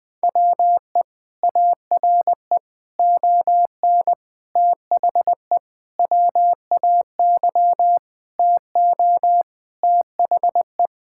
Morse code, heard as a keyed tone, about 20 wpm, 700 Hz. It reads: WE ARE ON THE WAY TO THE
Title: Sentences with 4-Word Phrases - 20wpm